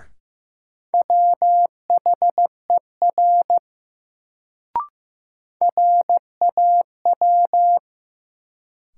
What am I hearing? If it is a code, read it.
Morse code, heard as a keyed tone, about 15 wpm, 700 Hz. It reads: WHER RAW
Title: Sentences from Top 1500 Words - New Word Drill - 15wpm